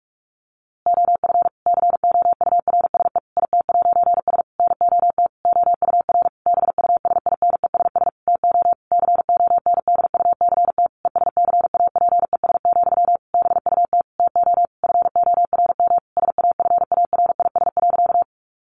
Morse code, heard as a keyed tone, about 44 wpm, 700 Hz. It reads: OF COURSE IT'S NOT OUR BUSINESS TO CONDUCT ESCAPES, BUT TO FORM SAFARIS.
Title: A Sound Of Thunder 45wpm